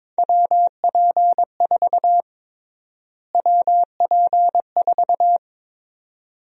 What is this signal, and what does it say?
Morse code, heard as a keyed tone, about 22 wpm, 700 Hz. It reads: WP4 WP4
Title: USA Call Sign Prefixes - Repeated 2x - 22wpm